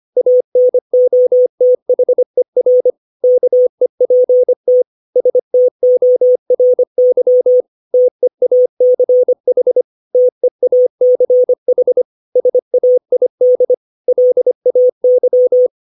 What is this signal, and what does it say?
Morse code, heard as a keyed tone, about 25 wpm, 500 Hz. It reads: ANOTHER KEPT STORY TEACH TEACH SAID LAY
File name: 632 words / 632 Words 25WPM